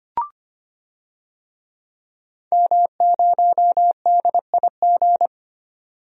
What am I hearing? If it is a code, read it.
Morse code, heard as a keyed tone, about 25 wpm, 700 Hz. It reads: M0DIG
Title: DXCC-100 Call Signs - 5 Characters - 25wpm